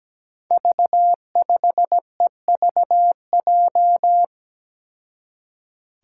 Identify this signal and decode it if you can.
Morse code, heard as a keyed tone, about 17 wpm, 700 Hz. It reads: V5EVJ
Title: Five Challenging Letter-Number Combinations 17wpm